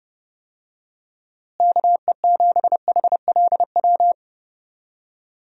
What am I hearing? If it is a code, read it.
Morse code, heard as a keyed tone, about 32 wpm, 700 Hz. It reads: KE7HLW
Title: USA Call Signs - 6 Characters - Repeated 2x - 30wpm